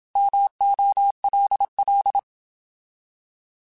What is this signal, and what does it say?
Morse code, heard as a keyed tone, about 27 wpm, 800 Hz. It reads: MOLL
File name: Alphabetical / Alphabetical mp3s / Ms @25